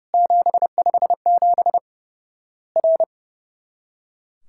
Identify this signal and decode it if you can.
Morse code, heard as a keyed tone, about 30 wpm, 700 Hz. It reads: 757 R
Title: Worked All Europe DX Contest - QTC Exchange - 30wpm